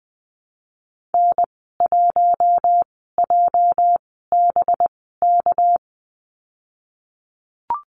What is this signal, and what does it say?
Morse code, heard as a keyed tone, about 20 wpm, spaced as for 15 wpm, 700 Hz. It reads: N1JBK
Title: 219 Lesson 35g - Number 0 - Call Signs 20wpm